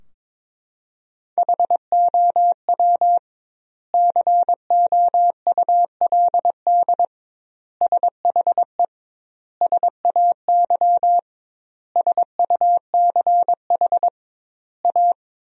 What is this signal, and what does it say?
Morse code, heard as a keyed tone, about 22 wpm, 700 Hz. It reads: HOW COULD SHE SAY SUCH A
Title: 2x Word-Spacing - Sentences from Top 200 Words - 22wpm